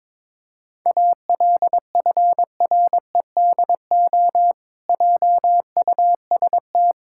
Morse code, heard as a keyed tone, about 22 wpm, 700 Hz. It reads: ALFREDO JUST
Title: Pangram Challenge v2 - 22wpm